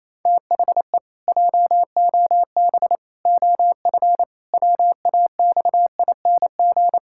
Morse code, heard as a keyed tone, about 28 wpm, 700 Hz. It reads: THE JOB OF WAXING
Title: Pangram Challenge v2 - 28wpm